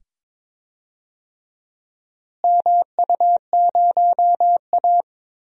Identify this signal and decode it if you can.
Morse code, heard as a keyed tone, about 21 wpm, 700 Hz. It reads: MU0A
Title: DXCC-100 Call Signs - 4 Characters - Repeated 2x - 22wpm